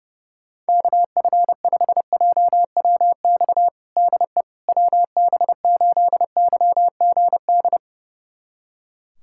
Morse code, heard as a keyed tone, about 30 wpm, 700 Hz. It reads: KF5JWX DE WB8YGD
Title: USA Callsign DE Callsign 30wpm